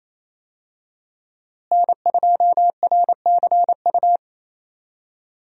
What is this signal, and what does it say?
Morse code, heard as a keyed tone, about 30 wpm, 700 Hz. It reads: N2RCU
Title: USA Call Signs - 5 Characters - Repeated 2x - 28wpm